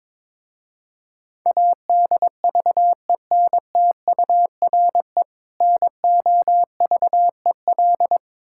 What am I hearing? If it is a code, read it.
Morse code, heard as a keyed tone, about 22 wpm, 700 Hz. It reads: ADVENTURE NOVEL
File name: Difficult Words with Hints - 22wpm